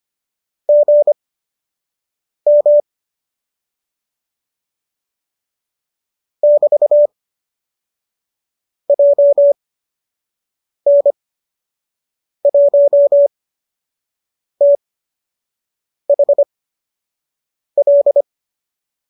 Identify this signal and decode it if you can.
Morse code, heard as a keyed tone, about 25 wpm, spaced as for 6 wpm, 570 Hz. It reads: GM <BT> JN1THL
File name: qso003 06